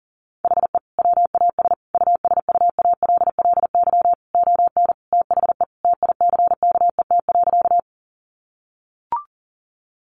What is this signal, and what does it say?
Morse code, heard as a keyed tone, about 37 wpm, 700 Hz. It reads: HE WAS USUALLY ON THE TICKET.
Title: Sentences from Top 1400 Words 40wpm